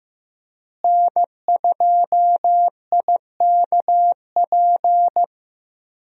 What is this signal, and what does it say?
Morse code, heard as a keyed tone, about 15 wpm, 700 Hz. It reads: N2IKP
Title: USA Call Signs - 5 Characters - Repeated 3x - 15wpm